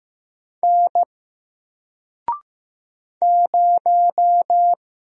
Morse code, heard as a keyed tone, about 15 wpm, 700 Hz. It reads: N 0
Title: Single Letter-Number Alt 15wpm